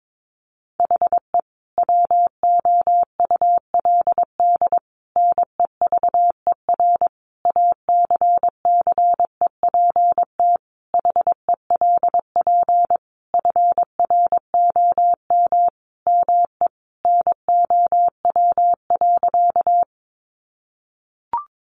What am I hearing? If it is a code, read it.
Morse code, heard as a keyed tone, about 22 wpm, 700 Hz. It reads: HE WOULD NEVER ACCEPT HELP FROM ME NOW.
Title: Sentences from Top 1300 Words 22wpm